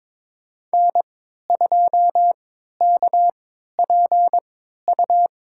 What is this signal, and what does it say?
Morse code, heard as a keyed tone, about 22 wpm, spaced as for 13 wpm, 700 Hz. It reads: N2KPU
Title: USA Call Signs - 3x Character Spacing - 22wpm